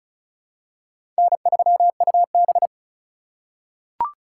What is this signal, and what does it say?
Morse code, heard as a keyed tone, about 36 wpm, 700 Hz. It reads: N3UB
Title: USA Call Signs - 4 Characters - 8x Character Spacing - 35wpm